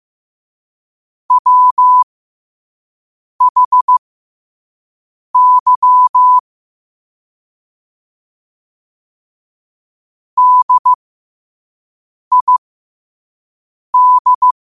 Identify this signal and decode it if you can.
Morse code, heard as a keyed tone, about 15 wpm, spaced as for 5 wpm, 1000 Hz. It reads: WHY DID
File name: 2026-03-16-5wpm